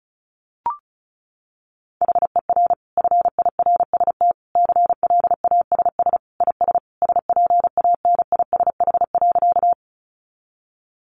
Morse code, heard as a keyed tone, about 34 wpm, 700 Hz. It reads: HER FIRST CLASS IS SPANISH.
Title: Additional Easy Phrases 35wpm